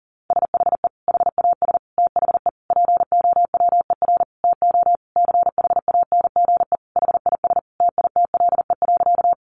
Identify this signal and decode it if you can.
Morse code, heard as a keyed tone, about 40 wpm, 700 Hz. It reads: SHE HAS THE POWER TO CHANGE HIS TITLE.
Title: Sentences from Top 1200 Words 40wpm